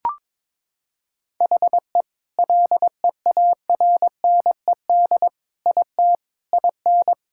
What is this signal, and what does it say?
Morse code, heard as a keyed tone, about 22 wpm, 700 Hz. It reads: HE LEARNED IT IN
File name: Sentences with 7-Word Phrases - 22wpm